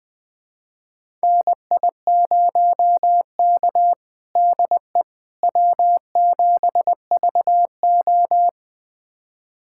Morse code, heard as a keyed tone, about 21 wpm, 700 Hz. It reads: NI0K DE W7VO
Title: USA Callsign DE Callsign - 4 Characters - 20wpm